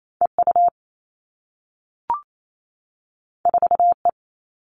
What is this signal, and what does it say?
Morse code, heard as a keyed tone, about 28 wpm, 700 Hz. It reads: EU 4E
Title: All Cut Numbers - 2-Digit 28wpm